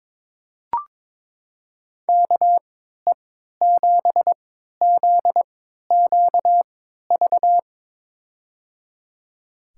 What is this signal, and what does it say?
Morse code, heard as a keyed tone, about 22 wpm, spaced as for 13 wpm, 700 Hz. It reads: KE7ZQV
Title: USA Call Signs - 3x Character Spacing - 22wpm